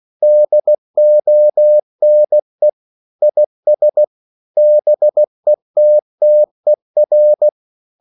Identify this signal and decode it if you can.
Morse code, heard as a keyed tone, about 16 wpm, 600 Hz. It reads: DONE IS BETTER
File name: CW / Practice / EnglishQuotations0416